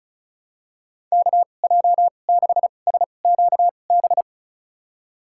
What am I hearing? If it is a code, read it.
Morse code, heard as a keyed tone, about 34 wpm, spaced as for 25 wpm, 700 Hz. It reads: KJ6SQB
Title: USA Call Signs - 6 Characters - 2x Character Spacing - 35wpm